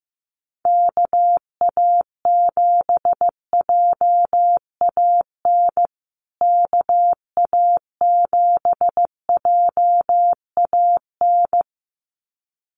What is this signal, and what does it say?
Morse code, heard as a keyed tone, about 15 wpm, 700 Hz. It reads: KA7JAN KA7JAN
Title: USA Call Signs Repeated 2x 15wpm